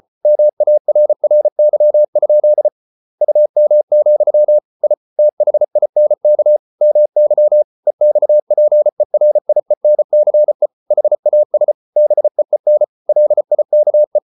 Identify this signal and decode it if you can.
Morse code, heard as a keyed tone, about 34 wpm, 600 Hz. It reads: MARRY? UM, I THINK MY EXPERIENCE HAS BEEN LIKE